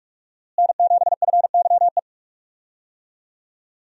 Morse code, heard as a keyed tone, about 47 wpm, 700 Hz. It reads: N7FYE
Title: USA Call Signs - 5 Characters - Repeated 2x - 45wpm